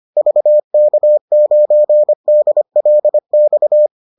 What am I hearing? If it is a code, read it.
Morse code, heard as a keyed tone, about 25 wpm, 600 Hz. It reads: VK9DLX